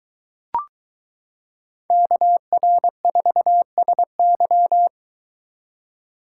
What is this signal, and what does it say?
Morse code, heard as a keyed tone, about 22 wpm, 700 Hz. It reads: KR4SY